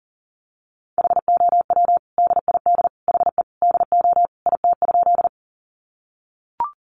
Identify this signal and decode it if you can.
Morse code, heard as a keyed tone, about 40 wpm, 700 Hz. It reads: HOW DID HE DO IT?
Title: Sentences from Top 100 Words 40wpm